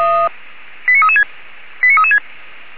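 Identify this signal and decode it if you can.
Five-tone selective call (ZVEI), heard as quick runs of five tones, tones of about 70 ms. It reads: 89336 89336
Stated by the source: ZVEI five-tone selective call received via SDR.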